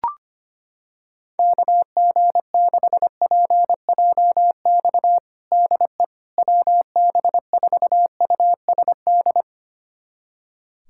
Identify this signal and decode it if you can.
Morse code, heard as a keyed tone, about 25 wpm, 700 Hz. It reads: KG6PJX DE WB4USD
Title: USA Callsign DE Callsign - 6 Characters - 25wpm